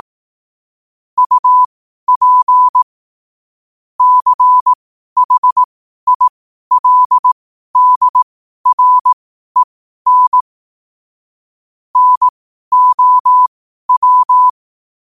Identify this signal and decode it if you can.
Morse code, heard as a keyed tone, about 18 wpm, spaced as for 12 wpm, 1000 Hz. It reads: UP CHILDREN NOW
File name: 12wpm